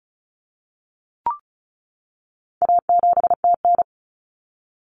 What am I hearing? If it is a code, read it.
Morse code, heard as a keyed tone, about 34 wpm, 700 Hz. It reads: A7TN
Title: Common Cut Numbers - 4-Digit 35wpm